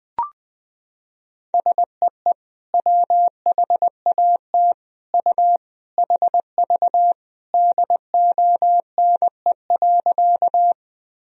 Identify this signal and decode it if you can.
Morse code, heard as a keyed tone, about 20 wpm, 700 Hz. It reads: SEE WHAT U HV DONE.
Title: Sentences from Top 500 Words and Abbreviations 20wpm